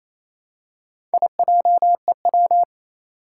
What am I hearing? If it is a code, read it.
Morse code, heard as a keyed tone, about 27 wpm, 700 Hz. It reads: IJEW
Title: Four Challenging Letter-Number Combinations 28wpm